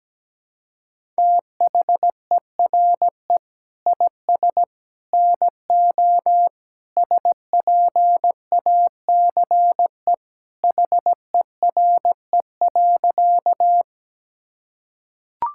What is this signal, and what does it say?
Morse code, heard as a keyed tone, about 17 wpm, 700 Hz. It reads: THERE IS NO SPACE HERE.
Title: Sentences from Top 1000 Words - Review Part 5 - 17wpm